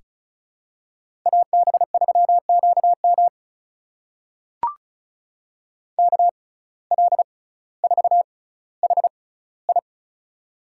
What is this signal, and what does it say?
Morse code, heard as a keyed tone, about 36 wpm, spaced as for 18 wpm, 700 Hz. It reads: AB3QM KL4HI
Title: USA Call Signs - 6x Character Spacing - 35wpm